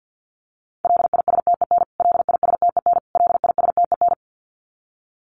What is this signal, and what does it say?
Morse code, heard as a keyed tone, about 50 wpm, 700 Hz. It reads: LISTEN LISTEN LISTEN
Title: Top 500 Words Repeated 50wpm